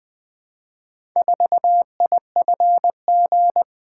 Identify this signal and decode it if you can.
Morse code, heard as a keyed tone, about 20 wpm, 700 Hz. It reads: 4IFG
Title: Four Letter-Number Combinations - 20wpm